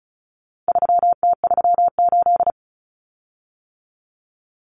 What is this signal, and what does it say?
Morse code, heard as a keyed tone, about 34 wpm, 700 Hz. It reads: 3T38
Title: Common Cut Numbers - 4-Digit 35wpm